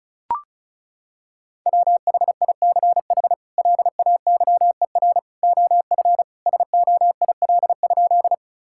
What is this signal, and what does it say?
Morse code, heard as a keyed tone, about 36 wpm, 700 Hz. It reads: WHICH LAYER OF SOIL?
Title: Sentences from Top 1600 Words 35wpm